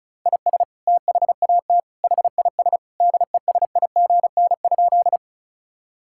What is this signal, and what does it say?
Morse code, heard as a keyed tone, about 36 wpm, 700 Hz. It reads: IS THAT HIS DESIGN?
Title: Sentences from Top 1000 Words - Review Part 5 - 35wpm